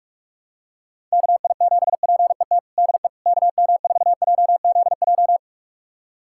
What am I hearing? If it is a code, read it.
Morse code, heard as a keyed tone, about 47 wpm, 700 Hz. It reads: KI7PET DE KM4JZJ
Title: USA Callsign DE Callsign - 6 Characters - 45wpm